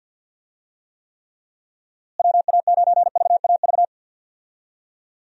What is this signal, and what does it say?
Morse code, heard as a keyed tone, about 52 wpm, 700 Hz. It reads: WA9VAV
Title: USA Call Signs - 6 Characters - Repeated 3x - 50wpm